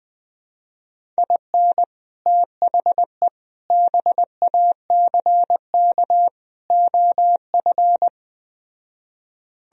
Morse code, heard as a keyed tone, about 20 wpm, 700 Hz. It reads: IN THE BACK OF
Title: Speed-Racing - Sets of 4 Words 17wpm